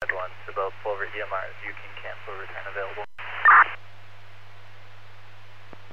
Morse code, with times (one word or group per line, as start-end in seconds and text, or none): none